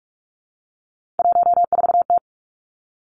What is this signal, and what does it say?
Morse code, heard as a keyed tone, about 46 wpm, 700 Hz. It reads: 14T
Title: Three Challenging Letter-Number Combinations 45wpm